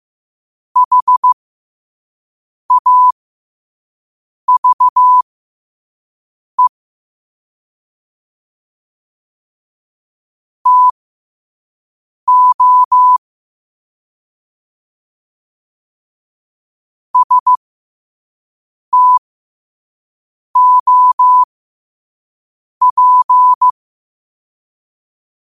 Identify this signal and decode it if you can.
Morse code, heard as a keyed tone, about 15 wpm, spaced as for 5 wpm, 1000 Hz. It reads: HAVE TO STOP
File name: qotdfiles / 5wpm